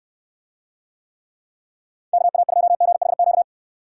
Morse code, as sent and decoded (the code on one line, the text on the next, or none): -.- .- ..--- --. .... --.-
KA2GHQ